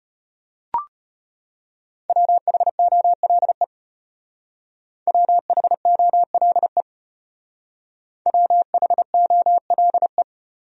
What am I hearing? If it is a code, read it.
Morse code, heard as a keyed tone, about 32 wpm, 700 Hz. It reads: WHOLE WHOLE WHOLE